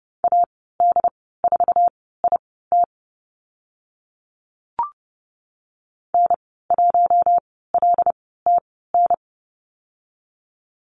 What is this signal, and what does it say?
Morse code, heard as a keyed tone, about 30 wpm, spaced as for 17 wpm, 700 Hz. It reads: AD4IT N1LTN
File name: Lesson 14g - Number 2 - Call Signs 30wpm